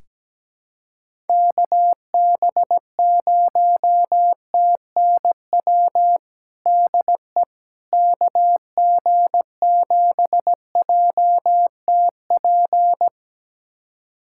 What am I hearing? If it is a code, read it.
Morse code, heard as a keyed tone, about 17 wpm, 700 Hz. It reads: KB0TNW DE KG7JTP
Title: USA Callsign DE Callsign - 6 Characters - 17wpm